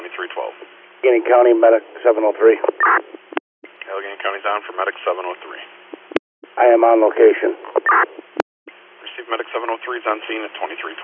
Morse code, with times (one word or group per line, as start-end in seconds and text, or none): none